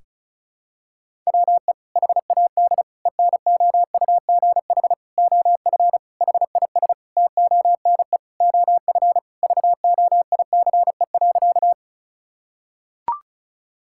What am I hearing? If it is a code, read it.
Morse code, heard as a keyed tone, about 34 wpm, 700 Hz. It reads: WE HAD ENOUGH OF HIS TONE OF VOICE.
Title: Sentences from Top 1600 Words 35wpm